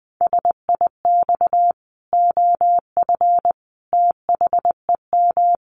Morse code, heard as a keyed tone, about 20 wpm, 700 Hz. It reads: SIX OF THEM
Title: Sentences from Top 1100 Words 20wpm